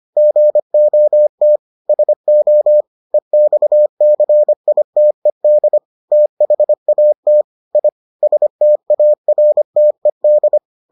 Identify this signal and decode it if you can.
Morse code, heard as a keyed tone, about 25 wpm, 600 Hz. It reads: GOT SO EXCITED THAT I STARTED